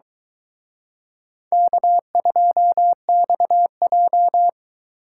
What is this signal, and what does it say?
Morse code, heard as a keyed tone, about 23 wpm, 700 Hz. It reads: K2XJ